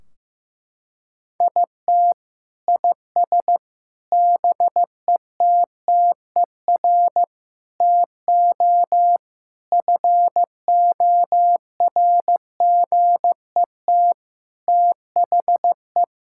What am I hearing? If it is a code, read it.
Morse code, heard as a keyed tone, about 15 wpm, 700 Hz. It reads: IT IS BETTER TO FORGET THE
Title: Sentences from Top 1600 Words 15wpm